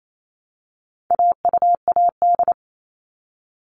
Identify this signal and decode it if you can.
Morse code, heard as a keyed tone, about 28 wpm, 700 Hz. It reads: AUAD
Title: All Cut Numbers - 4-Digit 28wpm